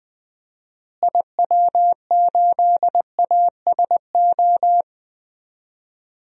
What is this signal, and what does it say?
Morse code, heard as a keyed tone, about 20 wpm, 700 Hz. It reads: IW8ASO
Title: DXCC-100 Call Signs - 6 Characters - Repeated 2x - 20wpm